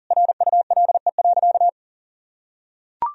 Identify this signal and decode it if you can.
Morse code, heard as a keyed tone, about 37 wpm, 700 Hz. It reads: RULE.
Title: Sentences from Top 1000 Words - Review Part 2 - 40wpm